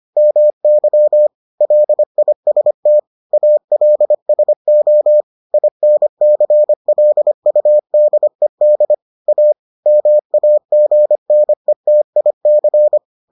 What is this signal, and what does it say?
Morse code, heard as a keyed tone, about 25 wpm, 600 Hz. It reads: MY LIST ALSO INCLUDED A MAGNETIC